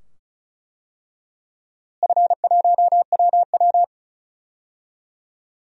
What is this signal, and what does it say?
Morse code, heard as a keyed tone, about 34 wpm, 700 Hz. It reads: F1WW
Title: DXCC-100 Call Signs - 4 Characters - Repeated 2x - 35wpm